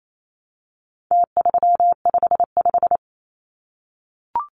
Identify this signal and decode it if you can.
Morse code, heard as a keyed tone, about 28 wpm, 700 Hz. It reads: T355
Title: Numbers - Contest Style - 4-Digit 28wpm